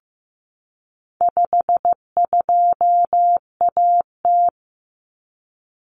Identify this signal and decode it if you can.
Morse code, heard as a keyed tone, about 15 wpm, 700 Hz. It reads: 52AT
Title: Common Cut Numbers - 4-Digit 15wpm